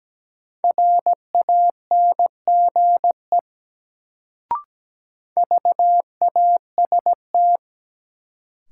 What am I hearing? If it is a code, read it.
Morse code, heard as a keyed tone, about 17 wpm, 700 Hz. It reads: RANGE VAST
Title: Sets of 4 Words - New Word Drill - 17wpm